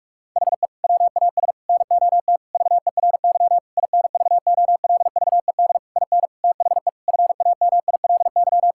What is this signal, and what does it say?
Morse code, heard as a keyed tone, about 42 wpm, 700 Hz. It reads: HE WAS NOT VERY INVOLVED IN THE FAMILY
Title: Sentences from Top 1000 Words - Review Part 5 - 45wpm